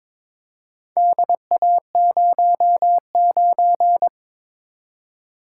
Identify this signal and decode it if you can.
Morse code, heard as a keyed tone, about 22 wpm, 700 Hz. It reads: DA09
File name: practice / 22wpm / Four Letter-Number Combinations - 22wpm